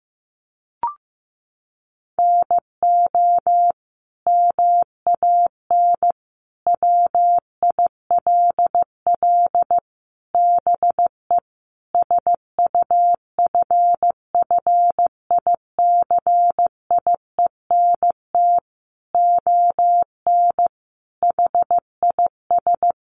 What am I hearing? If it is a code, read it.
Morse code, heard as a keyed tone, about 15 wpm, 700 Hz. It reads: NO MAN WILL BE SUFFICIENT ON HIS
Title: Sentences from Top 1400 Words 15wpm